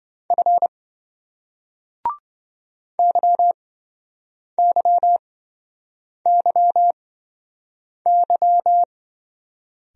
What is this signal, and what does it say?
Morse code, heard as a keyed tone, about 26 wpm, spaced as for 7 wpm, 700 Hz. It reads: F YYYY